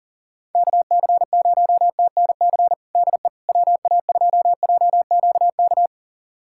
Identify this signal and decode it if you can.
Morse code, heard as a keyed tone, about 40 wpm, 700 Hz. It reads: KC0TNC DE WA2JQK
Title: USA Callsign DE Callsign - 6 Characters - 40wpm